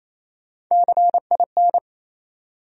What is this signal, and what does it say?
Morse code, heard as a keyed tone, about 28 wpm, 700 Hz. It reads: CIN
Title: USA Call Sign Postfixes - 3 Letters 28wpm